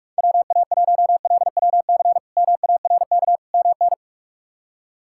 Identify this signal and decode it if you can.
Morse code, heard as a keyed tone, about 46 wpm, 700 Hz. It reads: WA1LWC MARK MN
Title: K1USN SST QSO Practice - 45wpm